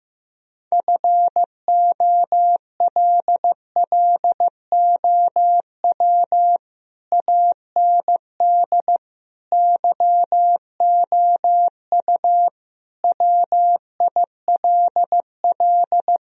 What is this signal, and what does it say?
Morse code, heard as a keyed tone, about 16 wpm, 700 Hz. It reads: FOLLOW AND YOU WILL
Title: Sentences from Top 1000 Words - Review Part 1 - 15wpm